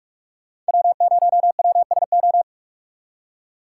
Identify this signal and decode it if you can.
Morse code, heard as a keyed tone, about 46 wpm, 700 Hz. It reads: W0WSO
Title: USA Call Signs - 5 Characters - 4x Character Spacing - 45wpm